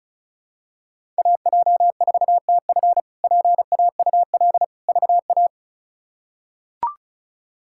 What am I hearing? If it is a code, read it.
Morse code, heard as a keyed tone, about 36 wpm, 700 Hz. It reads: AJ4TF PAUL VA
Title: K1USN SST QSO Practice - 35wpm